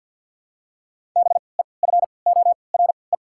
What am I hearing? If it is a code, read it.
Morse code, heard as a keyed tone, about 49 wpm, spaced as for 28 wpm, 700 Hz. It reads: BEFORE
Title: 3x Character Spacing - Top 300 Words - 50wpm